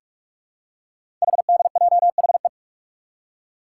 Morse code, heard as a keyed tone, about 44 wpm, 700 Hz. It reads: HDJHE
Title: Code Groups - 5 Letters - 45wpm